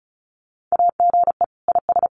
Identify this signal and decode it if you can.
Morse code, heard as a keyed tone, about 36 wpm, 700 Hz. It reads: AGE IS
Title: Common QSO Phrases Encore 35wpm